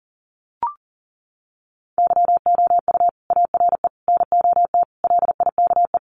Morse code, heard as a keyed tone, about 40 wpm, 700 Hz. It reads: YOU ARE NOT LIKE
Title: Sentences from Top 600 Words 40wpm